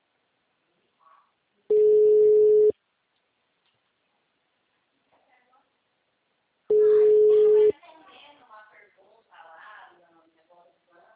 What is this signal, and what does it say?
Ringback tone (Europe, 425 Hz), with speech.